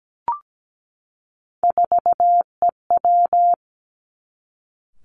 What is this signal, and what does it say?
Morse code, heard as a keyed tone, about 17 wpm, 700 Hz. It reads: 4EW
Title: Three Challenging Letter-Number Combinations 17wpm